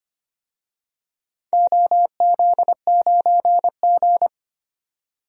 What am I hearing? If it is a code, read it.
Morse code, heard as a keyed tone, about 25 wpm, 700 Hz. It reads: OZ9G
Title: DXCC-100 Call Signs - 4 Characters - 25wpm